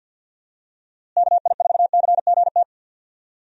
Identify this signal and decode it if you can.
Morse code, heard as a keyed tone, about 49 wpm, 700 Hz. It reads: KI4CGT
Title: USA Call Signs - 6 Characters - Repeated 2x - 50wpm